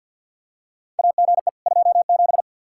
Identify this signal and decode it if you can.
Morse code, heard as a keyed tone, about 48 wpm, 700 Hz. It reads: AGE 27
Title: K1USN SST QSO Practice 50wpm